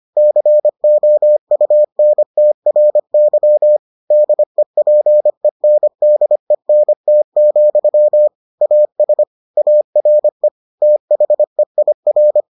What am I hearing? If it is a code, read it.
Morse code, heard as a keyed tone, about 25 wpm, 600 Hz. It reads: COUNTRY DEPENDENT, AS ARE THEIR